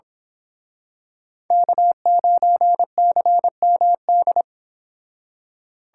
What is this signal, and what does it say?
Morse code, heard as a keyed tone, about 26 wpm, 700 Hz. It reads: K9CMD
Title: Speed Racing - USA 1x3 Call Signs 22wpm